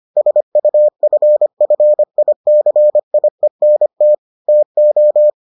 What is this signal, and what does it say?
Morse code, heard as a keyed tone, about 25 wpm, 600 Hz. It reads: SUFFICIENT TO